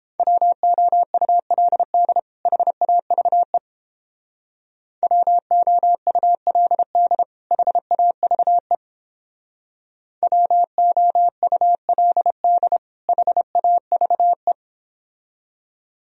Morse code, heard as a keyed tone, about 30 wpm, 700 Hz. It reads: WOULD HAVE WOULD HAVE WOULD HAVE